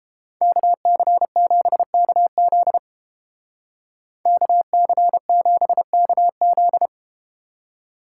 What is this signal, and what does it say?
Morse code, heard as a keyed tone, about 32 wpm, 700 Hz. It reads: KC7KZ KC7KZ